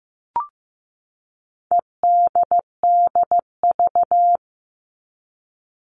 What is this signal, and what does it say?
Morse code, heard as a keyed tone, about 15 wpm, 700 Hz. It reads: EDDV
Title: All Cut Numbers - 4-Digit 15wpm